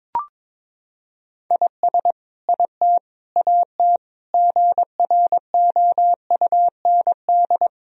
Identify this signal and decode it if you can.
Morse code, heard as a keyed tone, about 22 wpm, 700 Hz. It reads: IS IT AT GROUND
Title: Sentences from Top 1000 Words - Review Part 4 - 22wpm